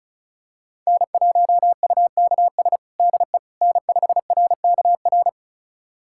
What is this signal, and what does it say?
Morse code, heard as a keyed tone, about 36 wpm, 700 Hz. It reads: N1UKS DE N5RKR
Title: USA Callsign DE Callsign - 5 Characters - 35wpm